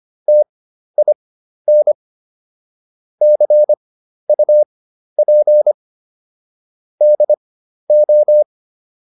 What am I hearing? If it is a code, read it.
Morse code, heard as a keyed tone, about 25 wpm, spaced as for 12 wpm, 600 Hz. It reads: TIN CUP DO